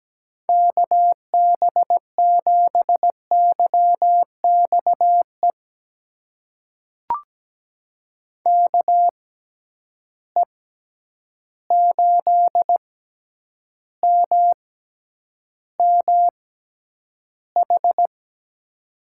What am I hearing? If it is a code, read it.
Morse code, heard as a keyed tone, about 17 wpm, spaced as for 9 wpm, 700 Hz. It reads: KB7YXE KE8MMH
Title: USA Call Signs - 6 Characters - 6x Character Spacing - 17wpm